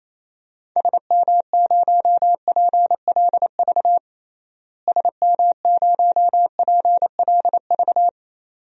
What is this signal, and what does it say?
Morse code, heard as a keyed tone, about 28 wpm, 700 Hz. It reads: SM0PLV SM0PLV
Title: DXCC-100 Call Signs - 6 Characters - Repeated 2x - 28wpm